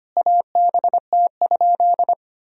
Morse code, heard as a keyed tone, about 25 wpm, 700 Hz. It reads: ABT?
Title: Sentences from Top 100 Words and Abbreviations 25wpm